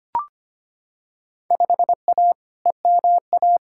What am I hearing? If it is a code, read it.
Morse code, heard as a keyed tone, about 25 wpm, 700 Hz. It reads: 5A EMA
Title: Field Day Exchanges Encore - 25wpm